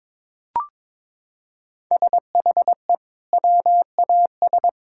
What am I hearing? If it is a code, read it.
Morse code, heard as a keyed tone, about 22 wpm, 700 Hz. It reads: SHE WAS
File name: practice / 22wpm / Sentences with 4-Word Phrases - 22wpm